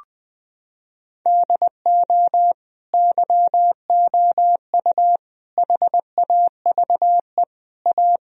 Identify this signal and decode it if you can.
Morse code, heard as a keyed tone, about 20 wpm, 700 Hz. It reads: DO YOU HAVE A
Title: Sentences from Top 1000 Words - Review Part 3 - 20wpm